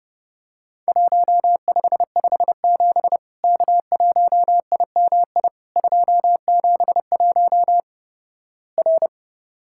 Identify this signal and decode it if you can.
Morse code, heard as a keyed tone, about 30 wpm, 700 Hz. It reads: 1557 K1IMI 271 R
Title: Worked All Europe DX Contest - QTC Exchange - 30wpm